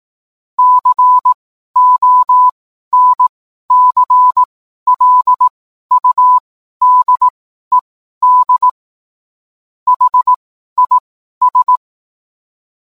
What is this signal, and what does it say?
Morse code, heard as a keyed tone, about 18 wpm, spaced as for 12 wpm, 1000 Hz. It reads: CONCLUDED HIS